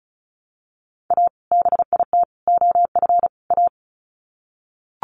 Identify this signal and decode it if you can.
Morse code, heard as a keyed tone, about 34 wpm, 700 Hz. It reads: A BIT OF A
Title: Sets of 4 Words 35wpm